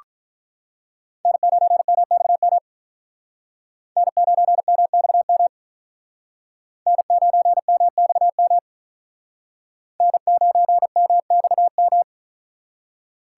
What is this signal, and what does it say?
Morse code, heard as a keyed tone, about 44 wpm, 700 Hz. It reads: N9MXM N9MXM N9MXM N9MXM